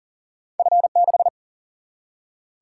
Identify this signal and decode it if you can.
Morse code, heard as a keyed tone, about 40 wpm, 700 Hz. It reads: F6
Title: Single-Letter Single-Number 40wpm